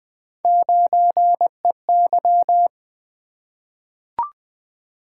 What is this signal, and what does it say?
Morse code, heard as a keyed tone, about 20 wpm, 700 Hz. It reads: 9EY
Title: Three Letter-Number Combinations - 20wpm